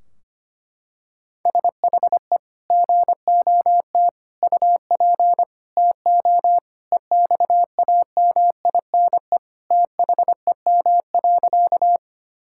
Morse code, heard as a keyed tone, about 25 wpm, 700 Hz. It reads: SHE GOT UP TO EXAMINE THEM.
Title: Sentences from Top 1600 Words 25wpm